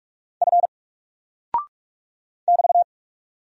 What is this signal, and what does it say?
Morse code, heard as a keyed tone, about 44 wpm, 700 Hz. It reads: F <BT>
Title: Single Character (Letter:Number:Punctuation:Prosign) 45wpm